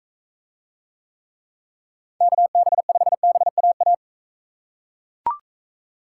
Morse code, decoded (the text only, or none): KB5BAA